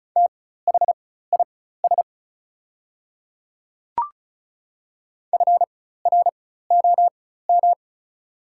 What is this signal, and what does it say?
Morse code, heard as a keyed tone, about 36 wpm, spaced as for 16 wpm, 690 Hz. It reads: THIS FROM
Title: 4x Character Spacing - Top 500 Words - 35wpm